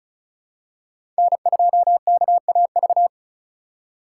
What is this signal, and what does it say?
Morse code, heard as a keyed tone, about 34 wpm, 700 Hz. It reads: N2KAV
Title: USA Call Signs - 5 Characters - 8x Character Spacing - 35wpm